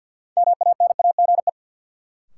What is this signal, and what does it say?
Morse code, heard as a keyed tone, about 52 wpm, 700 Hz. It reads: MANAGE
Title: Sentences from Top 1400 Words - New Word Drill - 50wpm